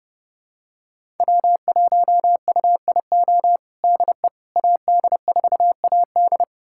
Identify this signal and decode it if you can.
Morse code, heard as a keyed tone, about 30 wpm, 700 Hz. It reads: W1UIO DE AD4AD
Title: USA Callsign DE Callsign - 5 Characters - 30wpm